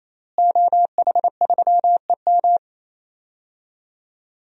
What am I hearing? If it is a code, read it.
Morse code, heard as a keyed tone, about 28 wpm, 700 Hz. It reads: OH3EM
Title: DXCC-100 Call Signs - 5 Characters - Repeated 3x - 28wpm